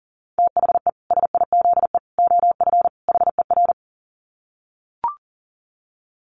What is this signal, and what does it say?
Morse code, heard as a keyed tone, about 40 wpm, 700 Hz. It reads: THE SIZE OF HER
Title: Dit Challenge 40wpm